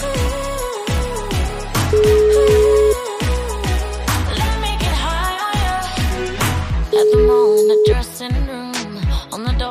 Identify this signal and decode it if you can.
Ringback tone (Europe, 425 Hz), with noise.